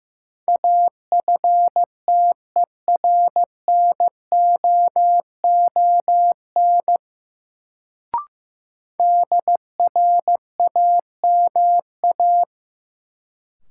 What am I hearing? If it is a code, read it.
Morse code, heard as a keyed tone, about 15 wpm, 700 Hz. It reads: AFTERNOON DRAMA
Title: Sentences from Top 1500 Words - New Word Drill - 15wpm